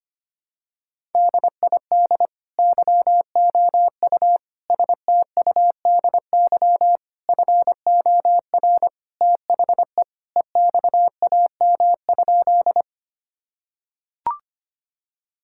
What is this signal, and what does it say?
Morse code, heard as a keyed tone, about 25 wpm, 700 Hz. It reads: DID YOU STUDY FOR THE EXAM?
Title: Sentences from Top 1000 Words - Review Part 5 - 25wpm